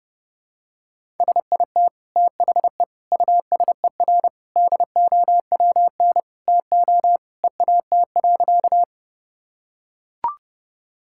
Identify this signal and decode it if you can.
Morse code, heard as a keyed tone, about 32 wpm, 700 Hz. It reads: SIT THE USER DOWN TO EAT.
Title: Sentences from Top 1000 Words - Review Part 5 - 30wpm